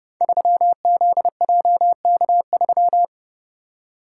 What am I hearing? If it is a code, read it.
Morse code, heard as a keyed tone, about 30 wpm, 700 Hz. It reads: 3ZJK3
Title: Code Groups - 5 Letter:Numbers - 30wpm